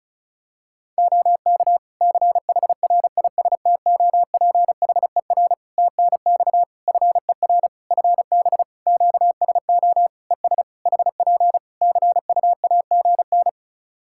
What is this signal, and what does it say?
Morse code, heard as a keyed tone, about 36 wpm, 700 Hz. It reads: OK CHRISTOPHER TNX FER FB QSO ES HP CUAGN
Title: Practice QSO - Exchange 4 - Ending - 35wpm